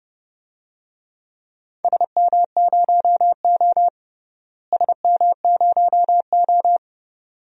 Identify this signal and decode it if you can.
Morse code, heard as a keyed tone, about 32 wpm, 700 Hz. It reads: SM0O SM0O
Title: DXCC-100 Call Signs - 4 Characters - Repeated 2x - 30wpm